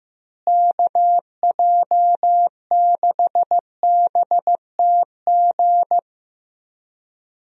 Morse code, heard as a keyed tone, about 15 wpm, 700 Hz. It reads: KJ6BTG
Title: USA Call Signs - 6 Characters - Repeated 3x - 15wpm